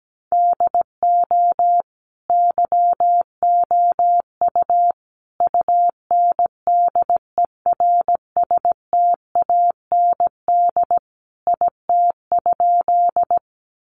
Morse code, heard as a keyed tone, about 17 wpm, 700 Hz. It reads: DO YOU UNDERSTAND IT?
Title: Sentences from Top 600 Words 17wpm